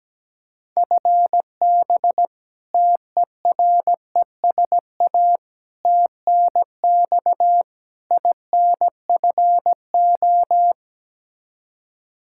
Morse code, heard as a keyed tone, about 17 wpm, 700 Hz. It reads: FB TERESA TNX INFO
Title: Practice QSO - Exchange 3 - More Info - 17wpm